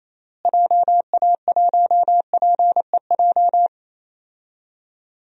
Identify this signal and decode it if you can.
Morse code, heard as a keyed tone, about 28 wpm, 700 Hz. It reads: JA1PEJ
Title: DXCC-100 Call Signs - 6 Characters - Repeated 3x - 28wpm